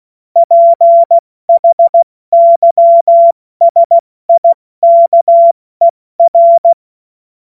Morse code, heard as a keyed tone, about 16 wpm, 680 Hz. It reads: PHYSIKER
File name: Morsekurs / Witze / Witz 12. April 2022 70 F80